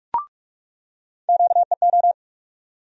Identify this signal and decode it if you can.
Morse code, heard as a keyed tone, about 47 wpm, 700 Hz. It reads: QEO
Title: USA Call Sign Postfixes - 3 Letters 45wpm